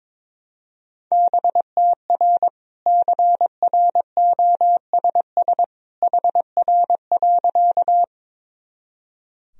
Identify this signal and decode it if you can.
Morse code, heard as a keyed tone, about 21 wpm, 700 Hz. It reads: BTR CROSS HR.
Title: Sentences from Top 400 Words and Abbreviations 22wpm